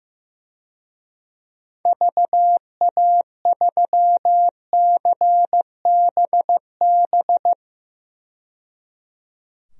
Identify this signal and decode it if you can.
Morse code, heard as a keyed tone, about 15 wpm, 700 Hz. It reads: VA3CBB
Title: DXCC-100 Call Signs - 6 Characters - 15wpm